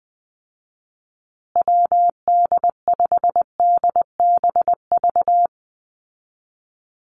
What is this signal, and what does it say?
Morse code, heard as a keyed tone, about 20 wpm, 700 Hz. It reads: WD5DBV
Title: Call Signs Repeated 2x - 20wpm